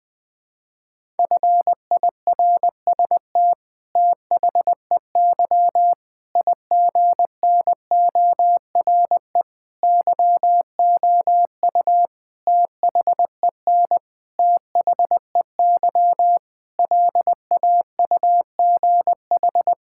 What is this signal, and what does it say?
Morse code, heard as a keyed tone, about 20 wpm, 700 Hz. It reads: FIRST THEY IGNORE YOU THEN THEY LAUGH
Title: Sets of 8 Words - 20wpm